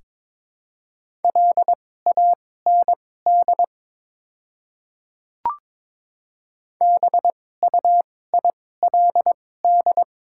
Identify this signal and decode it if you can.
Morse code, heard as a keyed tone, about 22 wpm, spaced as for 16 wpm, 700 Hz. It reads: LAND BUILD
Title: 2x Character Spacing - Top 300 Words - 22wpm